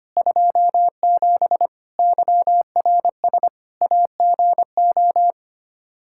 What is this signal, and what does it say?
Morse code, heard as a keyed tone, about 25 wpm, 700 Hz. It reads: 27 YRS AGO
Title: K1USN SST QSO Practice 25wpm